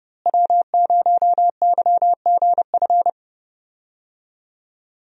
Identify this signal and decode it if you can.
Morse code, heard as a keyed tone, about 30 wpm, 700 Hz. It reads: W0YGF
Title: USA Call Signs - 5 Characters - Repeated 2x - 30wpm